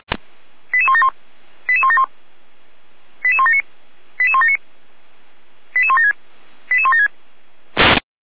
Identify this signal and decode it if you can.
Five-tone selective call (ZVEI), heard as quick runs of five tones, tones of about 70 ms. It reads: 88171 88171 88179 88179 88176 88176
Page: SDR/ZVEI